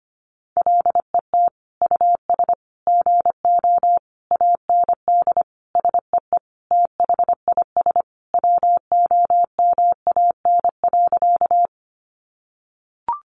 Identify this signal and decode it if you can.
Morse code, heard as a keyed tone, about 25 wpm, 700 Hz. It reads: LET US GO AND SEE THIS WOMAN.
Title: Sentences from Top 1100 Words 25wpm